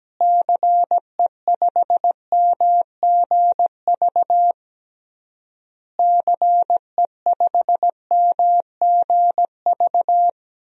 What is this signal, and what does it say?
Morse code, heard as a keyed tone, about 17 wpm, 700 Hz. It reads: CE5MGV CE5MGV
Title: DXCC-100 Call Signs - 6 Characters - Repeated 3x - 17wpm